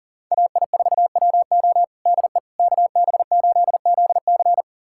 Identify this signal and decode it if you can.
Morse code, heard as a keyed tone, about 40 wpm, 700 Hz. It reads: AI4WO DE KB8ZC
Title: USA Callsign DE Callsign - 5 Characters - 40wpm